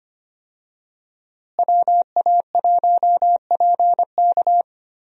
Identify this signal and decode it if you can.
Morse code, heard as a keyed tone, about 25 wpm, 700 Hz. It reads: WA1PK
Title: USA Call Signs - 5 Characters - Repeated 2x - 25wpm